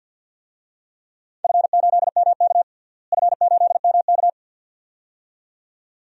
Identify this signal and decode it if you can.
Morse code, heard as a keyed tone, about 50 wpm, 700 Hz. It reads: F8MK F8MK
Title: DXCC-100 Call Signs - 4 Characters - Repeated 2x - 50wpm